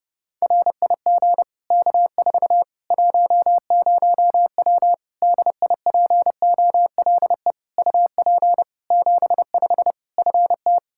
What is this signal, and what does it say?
Morse code, heard as a keyed tone, about 30 wpm, 700 Hz. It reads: RIG K4 10W DIPOLE UP 75 FT
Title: Practice QSO - Exchange 2 - Info - 30wpm